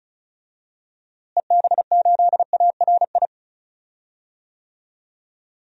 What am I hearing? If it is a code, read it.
Morse code, heard as a keyed tone, about 34 wpm, 700 Hz. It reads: EB8ARI
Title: DXCC-100 Call Signs - 6 Characters - Repeated 3x - 35wpm